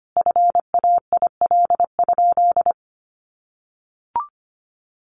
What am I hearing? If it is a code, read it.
Morse code, heard as a keyed tone, about 25 wpm, 700 Hz. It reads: FAIL?
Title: Sentences from Top 1100 Words 25wpm